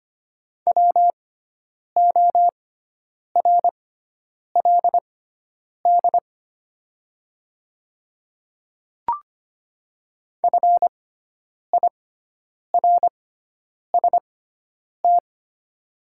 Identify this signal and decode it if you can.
Morse code, heard as a keyed tone, about 25 wpm, spaced as for 9 wpm, 700 Hz. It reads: WORLD FIRST
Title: 6x Character Spacing - Top 200 Words - 25wpm